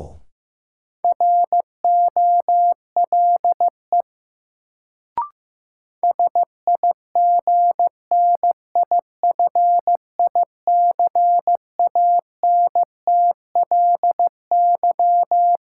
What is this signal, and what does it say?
Morse code, heard as a keyed tone, about 15 wpm, 700 Hz. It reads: ROLE SIGNIFICANTLY
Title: Sentences from Top 1500 Words - New Word Drill Alt - 15wpm